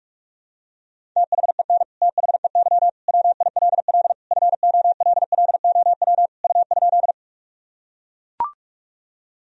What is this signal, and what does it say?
Morse code, heard as a keyed tone, about 41 wpm, 700 Hz. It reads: THEN THEY WILL FOLLOW U?
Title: Sentences from Top 200 Words and Abbreviations 45wpm